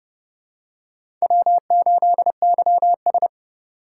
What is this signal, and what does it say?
Morse code, heard as a keyed tone, about 30 wpm, 690 Hz. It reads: W8YS
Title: USA Call Signs - 4 Characters - 2x Character Spacing - 30wpm